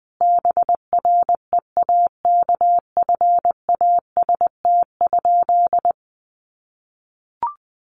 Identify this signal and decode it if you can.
Morse code, heard as a keyed tone, about 20 wpm, 700 Hz. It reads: BREAKFAST?
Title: Sentences from Top 1500 Words 20wpm